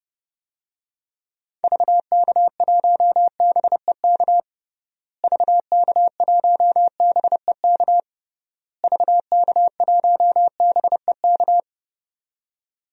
Morse code, heard as a keyed tone, about 30 wpm, 700 Hz. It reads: VK1BEK VK1BEK VK1BEK
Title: DXCC-100 Call Signs - 6 Characters - Repeated 3x - 30wpm